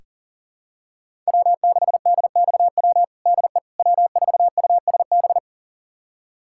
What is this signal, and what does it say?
Morse code, heard as a keyed tone, about 40 wpm, 700 Hz. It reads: W6DXW DE W4USB
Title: USA Callsign DE Callsign - 5 Characters - 40wpm